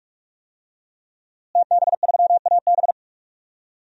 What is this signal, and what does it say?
Morse code, heard as a keyed tone, about 41 wpm, 700 Hz. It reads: TB3AB
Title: DXCC-100 Call Signs - 5 Characters - Repeated 3x - 45wpm